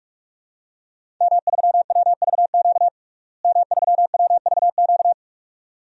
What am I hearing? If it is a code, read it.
Morse code, heard as a keyed tone, about 41 wpm, 700 Hz. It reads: M3WVQ M3WVQ
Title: DXCC-100 Call Signs - 5 Characters - Repeated 2x - 45wpm